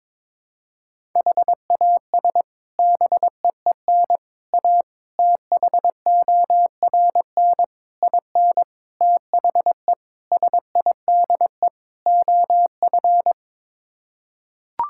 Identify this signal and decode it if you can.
Morse code, heard as a keyed tone, about 22 wpm, 700 Hz. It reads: HAS BEEN A THORN IN THE SIDE OF
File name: practice / 22wpm / Sets of 8 Words - 22wpm